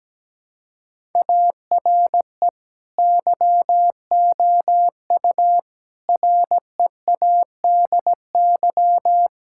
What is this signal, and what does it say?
Morse code, heard as a keyed tone, about 17 wpm, 700 Hz. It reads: ARE YOU READY
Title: Sentences from Top 1000 Words - Review Part 6 - 17wpm